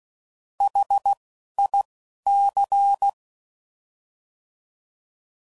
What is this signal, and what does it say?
Morse code, heard as a keyed tone, about 16 wpm, 800 Hz. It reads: H I C